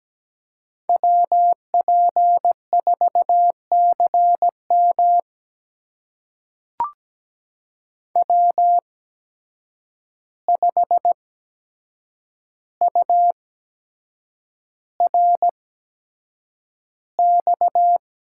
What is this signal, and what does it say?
Morse code, heard as a keyed tone, about 17 wpm, spaced as for 7 wpm, 700 Hz. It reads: WP4CM W5URX
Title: USA Call Signs - 8x Character Spacing - 17wpm